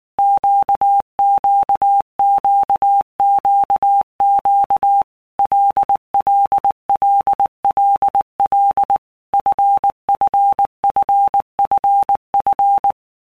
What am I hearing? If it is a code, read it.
Morse code, heard as a keyed tone, about 19 wpm, 800 Hz. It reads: QQQQQ LLLLL FFFFF
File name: Basic Letter Groups By Blocks / Groups@18 / LFCYQ@18